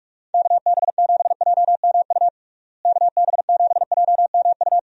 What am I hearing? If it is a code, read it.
Morse code, heard as a keyed tone, about 46 wpm, 700 Hz. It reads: KB7JMU KB7JMU
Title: USA Call Signs - 6 Characters - Repeated 3x - 45wpm